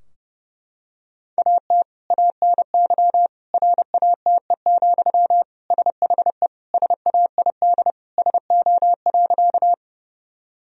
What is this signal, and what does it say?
Morse code, heard as a keyed tone, about 30 wpm, 700 Hz. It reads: AT ANY RATE, SHE SAID SO.
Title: Sentences from Top 1000 Words - Review Part 4 - 30wpm